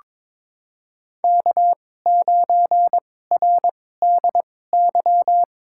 Morse code, heard as a keyed tone, about 22 wpm, spaced as for 16 wpm, 700 Hz. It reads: K9RDY
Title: USA Call Signs - 5 Characters - 2x Character Spacing - 22wpm